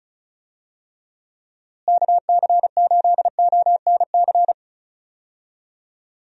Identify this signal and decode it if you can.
Morse code, heard as a keyed tone, about 36 wpm, 700 Hz. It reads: KC8ONC
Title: USA Call Signs - 6 Characters - Repeated 3x - 35wpm